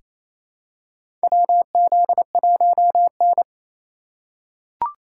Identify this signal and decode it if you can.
Morse code, heard as a keyed tone, about 28 wpm, 700 Hz. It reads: WZ1N
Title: USA Call Signs - 4 Characters - 4x Character Spacing - 28wpm